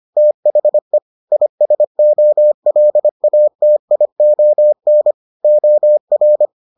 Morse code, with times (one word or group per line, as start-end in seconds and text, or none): 0.2-1.0 THE
1.3-5.1 ISOLATION
5.4-6.5 OR